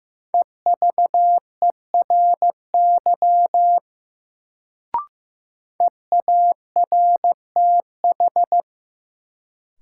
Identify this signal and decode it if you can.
Morse code, heard as a keyed tone, about 15 wpm, 700 Hz. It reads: EVERY EARTH
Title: Sentences from Top 200 Words - New Word Drill - 15wpm